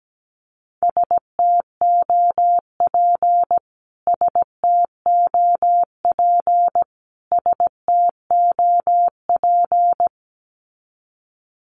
Morse code, heard as a keyed tone, about 17 wpm, 700 Hz. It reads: STOP STOP STOP
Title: Top 500 Words Repeated 17wpm